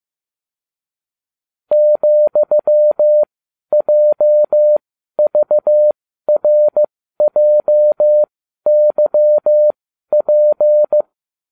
Morse code, heard as a keyed tone, about 15 wpm, spaced as for 12 wpm, 600 Hz. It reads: ,JVRJYP
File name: cw / ubacw20 2